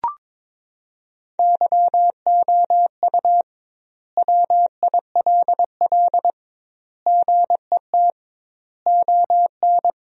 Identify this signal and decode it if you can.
Morse code, heard as a keyed tone, about 23 wpm, 700 Hz. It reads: YOU WILL GET ON
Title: 2x Word-Spacing - Sentences from Top 200 Words - 22wpm